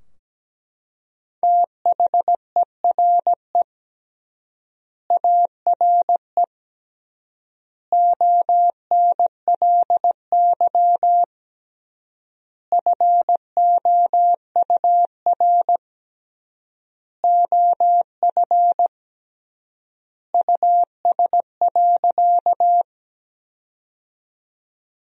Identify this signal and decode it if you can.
Morse code, heard as a keyed tone, about 17 wpm, 700 Hz. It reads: THERE ARE ONLY FOUR OF US.
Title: 3x Word-Spacing - Sentences from Top 300 Words - 17wpm